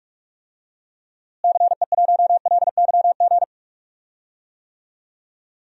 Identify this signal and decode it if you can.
Morse code, heard as a keyed tone, about 44 wpm, 700 Hz. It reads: CE1LYG
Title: DXCC-100 Call Signs - 6 Characters - Repeated 3x - 45wpm